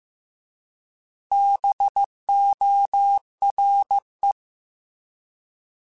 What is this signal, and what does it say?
Morse code, heard as a keyed tone, about 15 wpm, 800 Hz. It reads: BORE